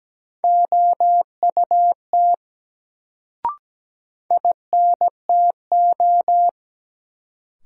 Morse code, heard as a keyed tone, about 17 wpm, 700 Hz. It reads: OUT INTO
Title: Sets of 2 Words - New Word Drill - 17wpm